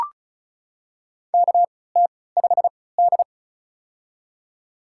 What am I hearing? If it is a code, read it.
Morse code, heard as a keyed tone, about 35 wpm, spaced as for 20 wpm, 700 Hz. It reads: KT5D
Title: USA Call Signs - 4 Characters - 3x Character Spacing - 35wpm